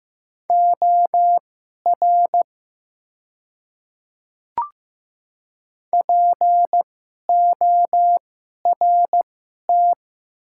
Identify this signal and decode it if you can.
Morse code, heard as a keyed tone, about 15 wpm, spaced as for 11 wpm, 700 Hz. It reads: OR PORT
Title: 2x Character Spacing - Top 300 Words - 15wpm